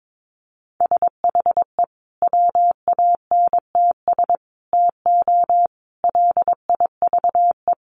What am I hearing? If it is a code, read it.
Morse code, heard as a keyed tone, about 22 wpm, 700 Hz. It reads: SHE WANTS TO LIVE
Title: Easy Phrases 22wpm